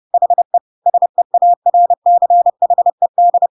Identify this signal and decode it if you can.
Morse code, heard as a keyed tone, about 30 wpm, 700 Hz. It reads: HE SEARCHED
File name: Pinocchio / Pinocchio ch1 30 30 ik1hge